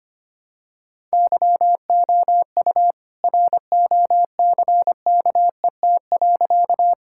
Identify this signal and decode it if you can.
Morse code, heard as a keyed tone, about 25 wpm, 700 Hz. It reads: YOU ROCKET.
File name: practice / 25wpm / All-Age Appropriate Jokes - 25wpm